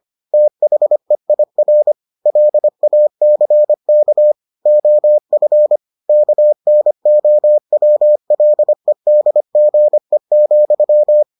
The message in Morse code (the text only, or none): THEIR LACK OF KNOWLEDGE,